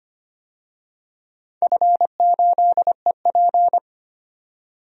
Morse code, heard as a keyed tone, about 25 wpm, 700 Hz. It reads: F8EP
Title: DXCC-100 Call Signs - 4 Characters - 25wpm